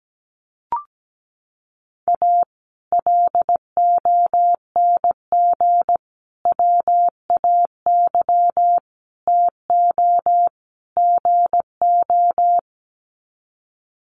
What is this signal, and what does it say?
Morse code, heard as a keyed tone, about 17 wpm, 700 Hz. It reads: A LONG WAY TO GO
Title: Sets of 5 Words 17wpm